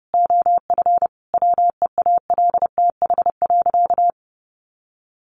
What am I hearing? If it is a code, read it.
Morse code, heard as a keyed tone, about 30 wpm, 700 Hz. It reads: OF WEALTH.
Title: Sentences from Top 1200 Words 30wpm